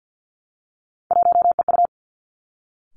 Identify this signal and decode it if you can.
Morse code, heard as a keyed tone, about 50 wpm, 700 Hz. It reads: 1EU
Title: Three Challenging Letter-Number Combinations 50wpm